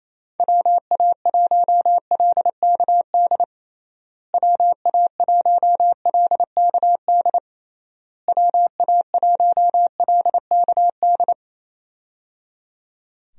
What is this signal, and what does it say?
Morse code, heard as a keyed tone, about 28 wpm, 700 Hz. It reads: WA1LKD WA1LKD WA1LKD
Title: USA Call Signs - 6 Characters - Repeated 3x - 28wpm